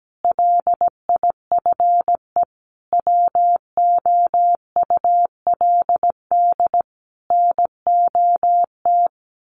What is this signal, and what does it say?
Morse code, heard as a keyed tone, about 17 wpm, 700 Hz. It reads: LIFE WOULD NOT
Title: Sentences from Top 1300 Words 17wpm